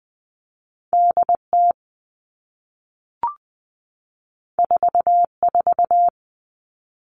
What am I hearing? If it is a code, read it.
Morse code, heard as a keyed tone, about 20 wpm, 700 Hz. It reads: DT 44
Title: All Cut Numbers - 2-Digit 20wpm